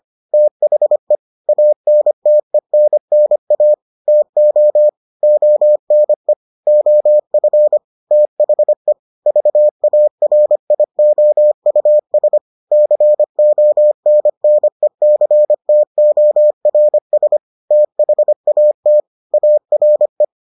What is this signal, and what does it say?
Morse code, heard as a keyed tone, about 25 wpm, 600 Hz. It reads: THE ANTENNA TO ONE OF THE VARIOUS CONNECTORS THAT ARE